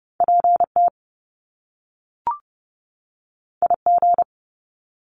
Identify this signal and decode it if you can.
Morse code, heard as a keyed tone, about 30 wpm, 700 Hz. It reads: PT IG
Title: Top Two Letter Combinations in Words 30wpm